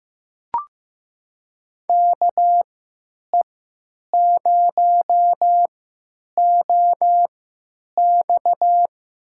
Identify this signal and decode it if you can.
Morse code, heard as a keyed tone, about 15 wpm, spaced as for 9 wpm, 700 Hz. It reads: KE0OX
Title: USA Call Signs - 3x Character Spacing - 15wpm